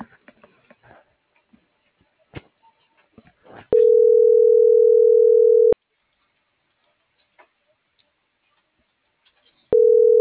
Ringback tone (North America), 440 + 480 Hz.